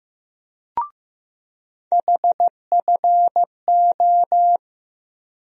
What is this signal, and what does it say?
Morse code, heard as a keyed tone, about 15 wpm, 700 Hz. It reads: HFO
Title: USA Call Sign Postfixes - 3 Letters 15wpm